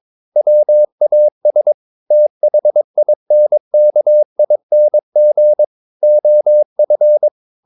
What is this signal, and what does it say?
Morse code, heard as a keyed tone, about 22 wpm, 600 Hz. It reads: WAS THINKING OF
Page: Romanen Never af Ken Follet på engelsk.